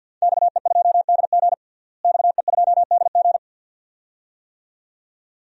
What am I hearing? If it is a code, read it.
Morse code, heard as a keyed tone, about 51 wpm, 700 Hz. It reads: XE2DG XE2DG
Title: DXCC-100 Call Signs - 5 Characters - Repeated 2x - 50wpm